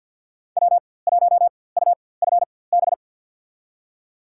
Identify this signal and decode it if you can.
Morse code, heard as a keyed tone, about 52 wpm, spaced as for 24 wpm, 700 Hz. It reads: W1UFB
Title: USA Call Signs - 4x Character Spacing - 50wpm